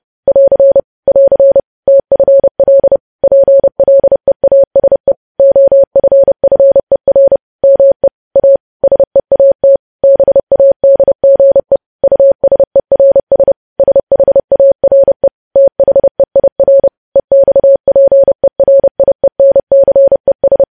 Morse code, heard as a keyed tone, about 30 wpm, 550 Hz. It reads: <AR> <AR> TFL PLEASE OFFER ME A SEAT BADGE USERS SHARE THEIR EXPERIENCES
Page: news headlines in morse code at 30 wpm